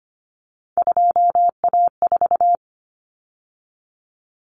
Morse code, heard as a keyed tone, about 25 wpm, 700 Hz. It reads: 2A4
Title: Common Cut Numbers - 3-Digit 25wpm